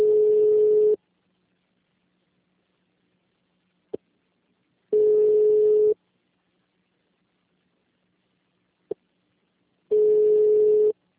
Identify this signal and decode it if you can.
Ringback tone (Europe, 425 Hz).
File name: out- -2010- - - .16734